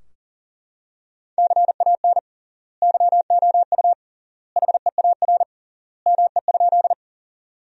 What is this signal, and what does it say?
Morse code, heard as a keyed tone, about 37 wpm, 700 Hz. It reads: CAN YOU HEAR ME?
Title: 3x Word-Spacing - Sentences from Top 300 Words - 40wpm